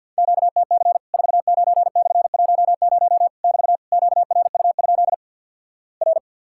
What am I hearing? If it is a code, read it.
Morse code, heard as a keyed tone, about 54 wpm, 700 Hz. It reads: QTC 49/10 <BT> QRU? R
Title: Worked All Europe DX Contest - QTC Exchange with Repeats - 50wpm